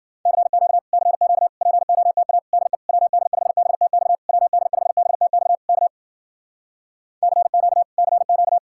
Morse code, heard as a keyed tone, about 52 wpm, 700 Hz. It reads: CQ CQ POTA DE WD5BTX WD5BTX K CQ CQ